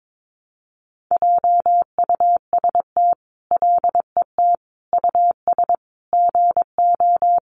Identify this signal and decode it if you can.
Morse code, heard as a keyed tone, about 22 wpm, 700 Hz. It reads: JUST LET US GO
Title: Sentences from Top 500 Words 22wpm